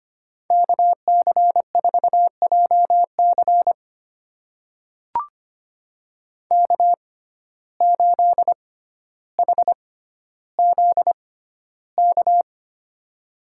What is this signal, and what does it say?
Morse code, heard as a keyed tone, about 25 wpm, spaced as for 13 wpm, 700 Hz. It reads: KC4JC K8HZK
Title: USA Call Signs - 6x Character Spacing - 25wpm